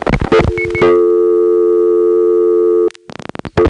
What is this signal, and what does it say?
Dial tone (North America), 350 + 440 Hz.
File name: phonetrips / SmyrnaGA3HQ